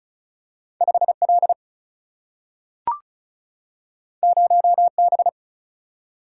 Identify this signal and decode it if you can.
Morse code, heard as a keyed tone, about 36 wpm, 700 Hz. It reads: 5L 0B
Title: Single-Letter Single-Number 35wpm